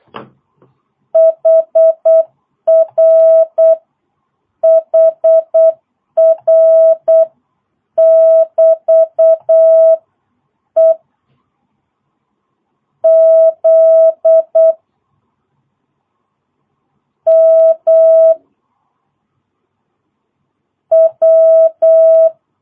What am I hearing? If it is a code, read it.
Morse code, heard as a keyed tone, about 8 wpm, 650 Hz. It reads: HR HR <BT> E Z M W
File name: jh8cbh / 2022cw